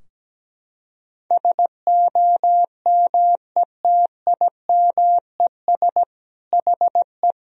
Morse code, heard as a keyed tone, about 17 wpm, 700 Hz. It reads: SOMETIMES HE
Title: Sentences from Top 1000 Words - Review Part 4 - 17wpm